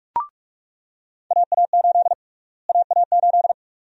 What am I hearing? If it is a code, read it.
Morse code, heard as a keyed tone, about 45 wpm, 700 Hz. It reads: AA8 AA8
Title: USA Call Sign Prefixes - Repeated 2x - 45wpm